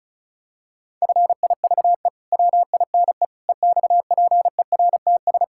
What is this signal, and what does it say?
Morse code, heard as a keyed tone, about 34 wpm, 700 Hz. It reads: FIVE WINE EXPERTS
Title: Pangram Challenge v2 - 35wpm